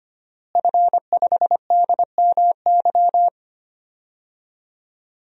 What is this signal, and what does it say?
Morse code, heard as a keyed tone, about 25 wpm, 700 Hz. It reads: F5DMY
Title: DXCC-100 Call Signs - 5 Characters - Repeated 2x - 25wpm